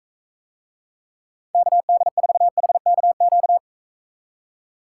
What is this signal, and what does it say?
Morse code, heard as a keyed tone, about 42 wpm, 700 Hz. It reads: KD4HKQ